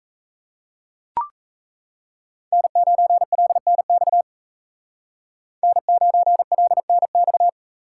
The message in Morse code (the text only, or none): N9LNX N9LNX